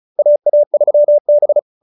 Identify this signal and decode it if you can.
Morse code, heard as a keyed tone, about 34 wpm, 590 Hz. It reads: AA3B
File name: QSO302 35